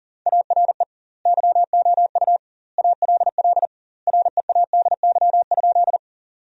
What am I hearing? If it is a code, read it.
Morse code, heard as a keyed tone, about 37 wpm, 700 Hz. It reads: ARE YOU ALL READY?
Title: 2x Word-Spacing - Sentences from Top 400 Words - 40wpm